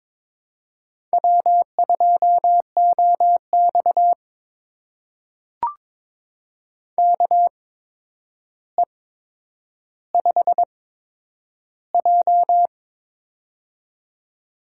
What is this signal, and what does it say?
Morse code, heard as a keyed tone, about 23 wpm, spaced as for 10 wpm, 700 Hz. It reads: W2OX KE5J
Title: USA Call Signs - 4 Characters - 8x Character Spacing - 22wpm